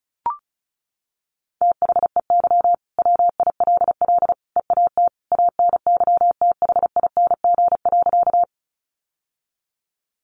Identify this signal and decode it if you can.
Morse code, heard as a keyed tone, about 34 wpm, 700 Hz. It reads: THEY WILL EAT ANYTHING.
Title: Sentences from Top 1000 Words 35wpm